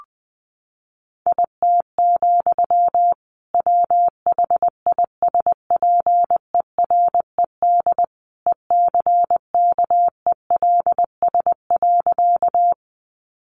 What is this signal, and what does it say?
Morse code, heard as a keyed tone, about 20 wpm, 700 Hz. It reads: IT, WHISPERED ECKELS.
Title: A Sound Of Thunder 20wpm